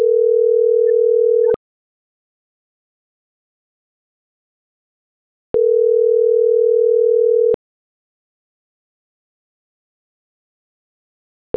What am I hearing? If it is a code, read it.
Ringback tone (North America), 440 + 480 Hz.